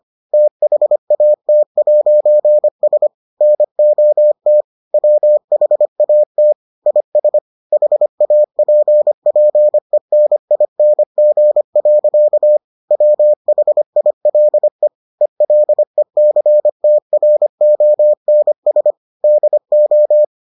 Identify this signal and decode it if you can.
Morse code, heard as a keyed tone, about 25 wpm, 600 Hz. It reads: THAT'S NOT WHAT IS HAPPENING. WHILE ELECTRONS DO